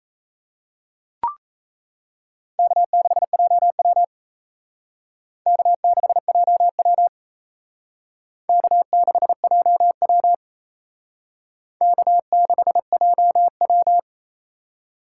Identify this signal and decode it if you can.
Morse code, heard as a keyed tone, about 35 wpm, 700 Hz. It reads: K6JW K6JW K6JW K6JW
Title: Speed Racing - USA Call Signs 28wpm